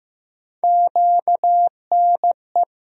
Morse code, heard as a keyed tone, about 15 wpm, 700 Hz. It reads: QNE
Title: USA Call Sign Postfixes - 3 Letters 15wpm